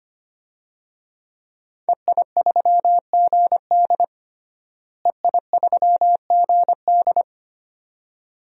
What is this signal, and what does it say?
Morse code, heard as a keyed tone, about 25 wpm, 700 Hz. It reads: EI3GD EI3GD
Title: DXCC-100 Call Signs - 5 Characters - Repeated 2x - 25wpm